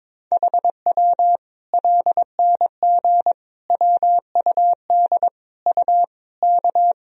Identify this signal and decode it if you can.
Morse code, heard as a keyed tone, about 22 wpm, 700 Hz. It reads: HW LNG WUD U K
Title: Sentences from Top 500 Words and Abbreviations 22wpm